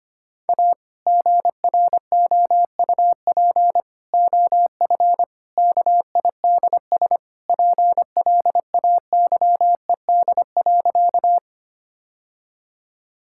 Morse code, heard as a keyed tone, about 25 wpm, 700 Hz. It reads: A GROUP OF KIDS PLAYED.
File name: practice / 25wpm / Sentences with 3-Word Phrases - 25wpm